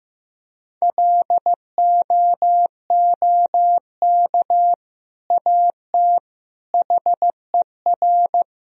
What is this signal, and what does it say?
Morse code, heard as a keyed tone, about 16 wpm, 700 Hz. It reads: LOOK AT HER
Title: Sentences from Top 1000 Words - Review Part 1 - 15wpm